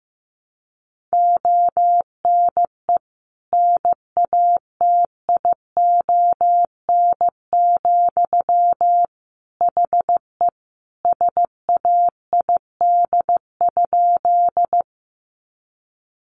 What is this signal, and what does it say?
Morse code, heard as a keyed tone, about 15 wpm, 700 Hz. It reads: ONE NATION, HE SAID?
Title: Sentences from Top 900 Words 15wpm